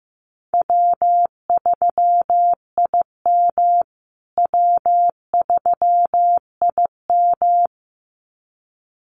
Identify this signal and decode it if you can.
Morse code, heard as a keyed tone, about 15 wpm, 700 Hz. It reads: W3IM W3IM
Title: USA Call Signs Repeated 2x 15wpm